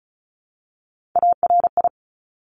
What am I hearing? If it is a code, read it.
Morse code, heard as a keyed tone, about 36 wpm, 690 Hz. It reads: ARI
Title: Top Three Letter Combinations in Words 35wpm